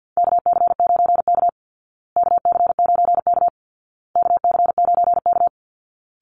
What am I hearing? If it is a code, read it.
Morse code, heard as a keyed tone, about 50 wpm, 700 Hz. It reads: KC9K KC9K KC9K
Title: Call Signs Repeated 2x - 50wpm